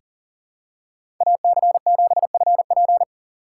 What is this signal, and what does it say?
Morse code, heard as a keyed tone, about 40 wpm, 700 Hz. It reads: AC7FP
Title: USA Call Signs - 2x Character Spacing - 40wpm